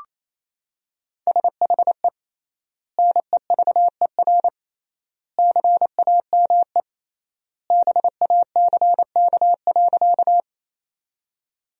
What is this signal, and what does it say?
Morse code, heard as a keyed tone, about 28 wpm, 700 Hz. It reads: SHE NEVER CAME BACK.
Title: 3x Word-Spacing - Sentences from Top 300 Words - 28wpm